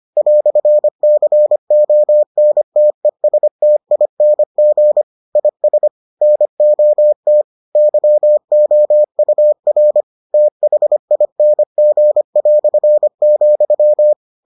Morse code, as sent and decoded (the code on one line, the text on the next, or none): .-..-. -.-. --- -. - . ... - .. -. --. / .. ... / -. --- - / -.-- --- ..- .-. / - .... .. -. --. .-..-. --..--
"CONTESTING IS NOT YOUR THING",